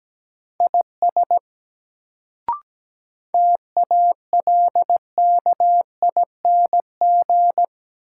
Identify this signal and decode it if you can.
Morse code, heard as a keyed tone, about 17 wpm, 700 Hz. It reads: IS TALKING
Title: Sets of 2 Words - New Word Drill - 17wpm